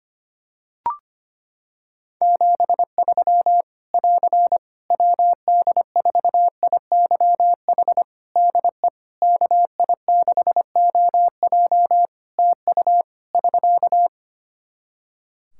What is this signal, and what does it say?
Morse code, heard as a keyed tone, about 25 wpm, 700 Hz. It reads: 73 <AR> WD4IYH DE KI6OJ TU <SK>
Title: K1USN SST QSO Practice 25wpm